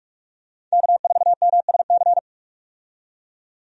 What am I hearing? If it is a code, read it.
Morse code, heard as a keyed tone, about 43 wpm, 700 Hz. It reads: K4MSC
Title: USA Call Signs - 5 Characters - 45wpm